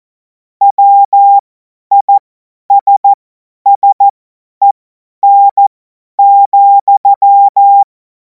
Morse code, heard as a keyed tone, about 14 wpm, spaced as for 10 wpm, 800 Hz. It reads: WISSEN,
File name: Morsekurs / Witze / Witz 10. März 2026 50 F70